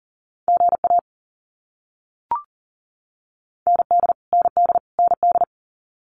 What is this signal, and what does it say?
Morse code, heard as a keyed tone, about 40 wpm, 700 Hz. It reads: GA ND ND ND
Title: US States Repeated 3x 40wpm